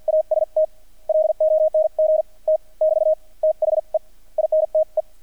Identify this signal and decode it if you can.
Morse code, heard as a keyed tone, about 48 wpm, spaced as for 41 wpm, 630 Hz. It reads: AST LONG T <BT> THE INTE
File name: archived / CRSnet